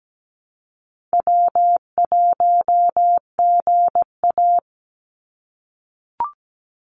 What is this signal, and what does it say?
Morse code, heard as a keyed tone, about 17 wpm, 700 Hz. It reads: W1GA
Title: USA Call Signs 17wpm